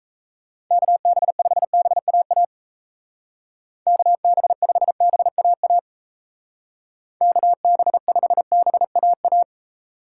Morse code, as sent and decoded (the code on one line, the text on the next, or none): -.- -... ..... -... .- .- / -.- -... ..... -... .- .- / -.- -... ..... -... .- .-
KB5BAA KB5BAA KB5BAA